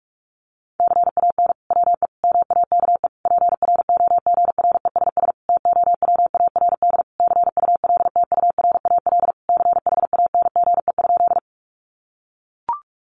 Morse code, heard as a keyed tone, about 44 wpm, 700 Hz. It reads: CAN WE MAKE PROGRESS TOWARD CULTURAL CHANGE?
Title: Sentences from Top 1500 Words 45wpm